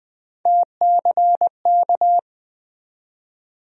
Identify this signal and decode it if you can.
Morse code, heard as a keyed tone, about 20 wpm, 700 Hz. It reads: TCK
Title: USA Call Sign Postfixes - 3 Letters 20wpm